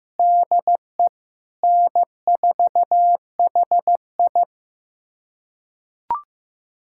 Morse code, heard as a keyed tone, about 15 wpm, 700 Hz. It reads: DE N4HI
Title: USA Callsign DE Callsign - 4 Characters - 15wpm